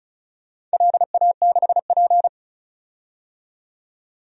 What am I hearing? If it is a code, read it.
Morse code, heard as a keyed tone, about 34 wpm, 700 Hz. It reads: LA6P
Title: DXCC-100 Call Signs - 4 Characters - 35wpm